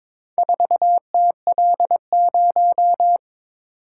Morse code, heard as a keyed tone, about 22 wpm, 700 Hz. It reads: 4TL0
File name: practice / 22wpm / Four Letter-Number Combinations - 22wpm